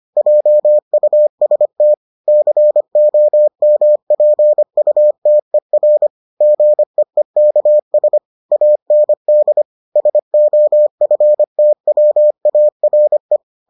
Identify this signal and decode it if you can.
Morse code, heard as a keyed tone, about 25 wpm, 600 Hz. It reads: JUST COMPUTER GEEKS AND SOFTWARE